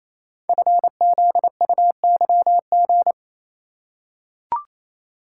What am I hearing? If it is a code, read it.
Morse code, heard as a keyed tone, about 28 wpm, 700 Hz. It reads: FZUYG
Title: Code Groups - 5 Letter:Numbers - 28wpm